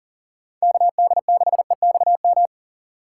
Morse code, heard as a keyed tone, about 40 wpm, 700 Hz. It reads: KD6EXM
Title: USA Call Signs - 6 Characters - 4x Character Spacing - 40wpm